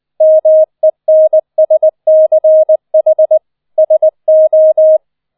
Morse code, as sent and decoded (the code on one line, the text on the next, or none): -- . -. ... -.-. .... / ... ---
MENSCH SO